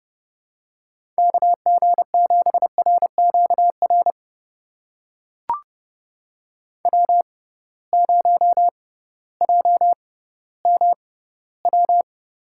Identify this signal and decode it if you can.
Morse code, heard as a keyed tone, about 30 wpm, spaced as for 16 wpm, 700 Hz. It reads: KG7RQR W0JMW
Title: USA Call Signs - 6x Character Spacing - 30wpm